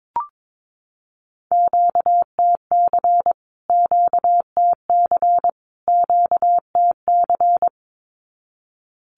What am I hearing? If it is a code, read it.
Morse code, heard as a keyed tone, about 22 wpm, 700 Hz. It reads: QTC QTC QTC
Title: Q-Codes Repeated 3x 22wpm